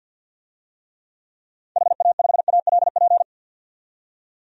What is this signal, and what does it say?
Morse code, heard as a keyed tone, about 52 wpm, 700 Hz. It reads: HA5ALP
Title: DXCC-100 Call Signs - 6 Characters - 50wpm